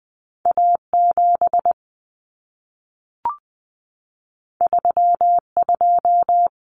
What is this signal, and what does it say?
Morse code, heard as a keyed tone, about 20 wpm, 700 Hz. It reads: A7 32
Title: Common Cut Numbers - 2-Digit 20wpm